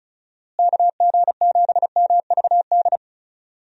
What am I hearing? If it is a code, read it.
Morse code, heard as a keyed tone, about 36 wpm, 700 Hz. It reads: KG7MVD
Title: USA Call Signs - 6 Characters - Repeated 2x - 35wpm